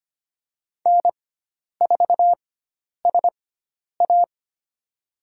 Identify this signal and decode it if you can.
Morse code, heard as a keyed tone, about 25 wpm, spaced as for 10 wpm, 700 Hz. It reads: N4SA
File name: Lesson 16g - Letter c - Call Signs 25wpm